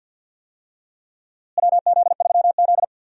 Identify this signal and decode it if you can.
Morse code, heard as a keyed tone, about 50 wpm, 700 Hz. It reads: WZ3Z
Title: USA Call Signs - 4 Characters - Repeated 2x - 50wpm